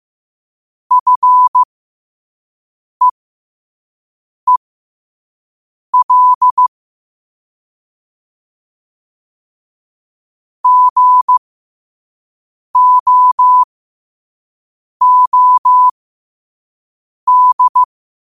Morse code, heard as a keyed tone, about 15 wpm, spaced as for 5 wpm, 1000 Hz. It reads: FEEL GOOD
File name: qotdfiles / 5wpm